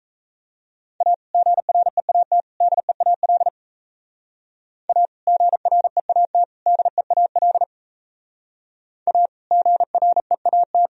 Morse code, heard as a keyed tone, about 34 wpm, 700 Hz. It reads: A GREAT DEAL A GREAT DEAL A GREAT
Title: Speed-Racing - Sets of 3 Words 28wpm